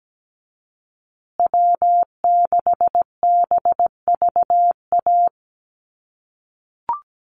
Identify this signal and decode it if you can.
Morse code, heard as a keyed tone, about 17 wpm, 700 Hz. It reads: W6BVA
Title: USA Call Signs Repeated 2x 17wpm